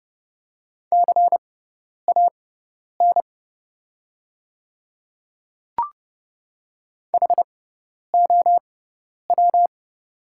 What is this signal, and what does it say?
Morse code, heard as a keyed tone, about 30 wpm, spaced as for 10 wpm, 700 Hz. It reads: CAN HOW
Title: 6x Character Spacing - Top 100 Words - 30wpm